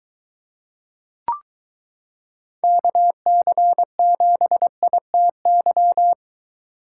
Morse code, heard as a keyed tone, about 23 wpm, 700 Hz. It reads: KC7ITY